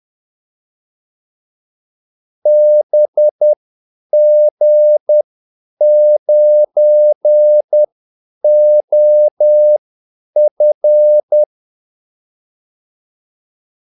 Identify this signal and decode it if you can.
Morse code, heard as a keyed tone, about 10 wpm, spaced as for 8 wpm, 600 Hz. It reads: BG9OF
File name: Callsigns-10-8wpm